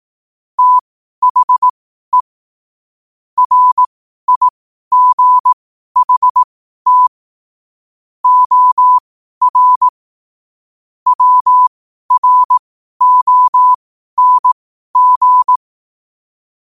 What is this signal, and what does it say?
Morse code, heard as a keyed tone, about 18 wpm, spaced as for 12 wpm, 1000 Hz. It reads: THE RIGHT OR WRONG
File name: qotdfiles / 12wpm